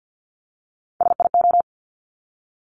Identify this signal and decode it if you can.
Morse code, heard as a keyed tone, about 50 wpm, 700 Hz. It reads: SIO
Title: Top Three Letter Combinations in Words 50wpm